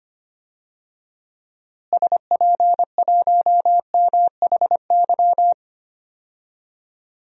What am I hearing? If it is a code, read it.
Morse code, heard as a keyed tone, about 25 wpm, 700 Hz. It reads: SP1MHY
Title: DXCC-100 Call Signs - 6 Characters - Repeated 3x - 25wpm